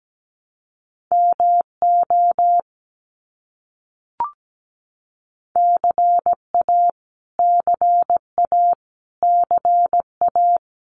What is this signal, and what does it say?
Morse code, heard as a keyed tone, about 17 wpm, 700 Hz. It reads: MO CA CA CA
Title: US States Repeated 3x 17wpm